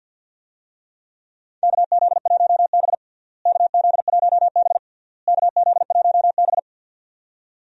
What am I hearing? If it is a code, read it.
Morse code, heard as a keyed tone, about 48 wpm, 700 Hz. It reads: KZ1B KZ1B KZ1B
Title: USA Call Signs - 4 Characters - Repeated 3x - 50wpm